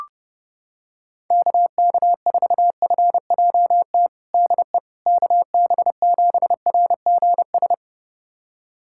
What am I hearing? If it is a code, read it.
Morse code, heard as a keyed tone, about 30 wpm, 700 Hz. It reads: KK4FJT DE KB7RGS
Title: USA Callsign DE Callsign - 6 Characters - 30wpm